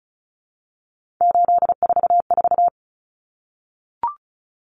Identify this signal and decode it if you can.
Morse code, heard as a keyed tone, about 34 wpm, 700 Hz. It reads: 844
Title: Numbers - 3-Digit 35wpm